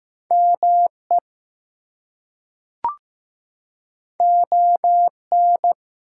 Morse code, heard as a keyed tone, about 15 wpm, 700 Hz. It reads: ME ON
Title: Two Letter Words - 15wpm